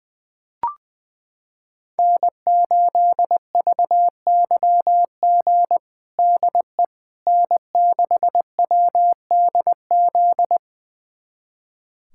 Morse code, heard as a keyed tone, about 21 wpm, 700 Hz. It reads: N8VYG DE N6WDZ
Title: USA Callsign DE Callsign - 5 Characters - 20wpm